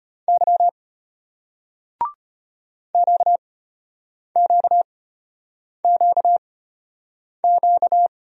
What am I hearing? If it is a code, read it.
Morse code, heard as a keyed tone, about 35 wpm, spaced as for 8 wpm, 700 Hz. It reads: Y QQQQ